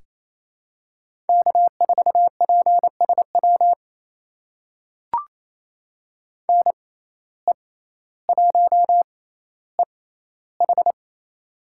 Morse code, heard as a keyed tone, about 28 wpm, spaced as for 14 wpm, 700 Hz. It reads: K4PSW NE1EH
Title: USA Call Signs - 5 Characters - 6x Character Spacing - 28wpm